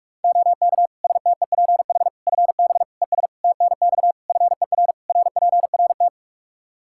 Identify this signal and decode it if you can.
Morse code, heard as a keyed tone, about 41 wpm, 700 Hz. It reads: OK STEPH FB ES TNX FER RPRT
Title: Practice QSO - Exchange 2 - Info - 45wpm